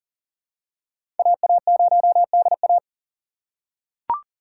Morse code, heard as a keyed tone, about 40 wpm, 690 Hz. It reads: AA0DA
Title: USA Call Signs - 5 Characters - 6x Character Spacing - 40wpm